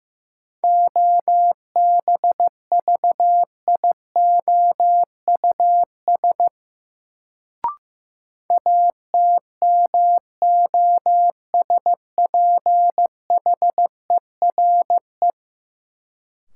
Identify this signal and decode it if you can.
Morse code, heard as a keyed tone, about 16 wpm, 700 Hz. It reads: OBVIOUS ATMOSPHERE
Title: Sentences from Top 1600 Words - New Word Drill Alt - 15wpm